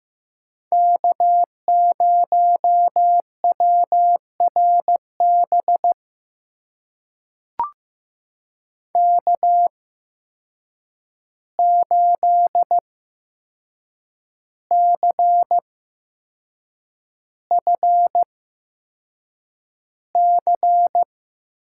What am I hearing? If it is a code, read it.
Morse code, heard as a keyed tone, about 15 wpm, spaced as for 6 wpm, 700 Hz. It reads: K0WRB K8CFC
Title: USA Call Signs - 5 Characters - 8x Character Spacing - 15wpm